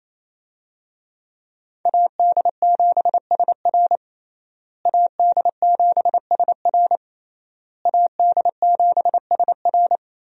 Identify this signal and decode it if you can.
Morse code, heard as a keyed tone, about 27 wpm, 700 Hz. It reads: AD7SR AD7SR AD7SR
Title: USA Call Signs - 5 Characters - Repeated 3x - 28wpm